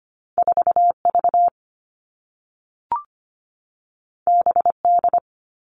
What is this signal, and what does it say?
Morse code, heard as a keyed tone, about 25 wpm, 700 Hz. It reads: 4 V B D
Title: All Cut Numbers - 2-Digit 25wpm